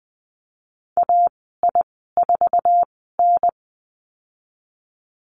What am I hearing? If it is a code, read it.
Morse code, heard as a keyed tone, about 20 wpm, spaced as for 15 wpm, 700 Hz. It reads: AI4N
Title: 069 Lesson 12g - Letter d - Call Signs 20wpm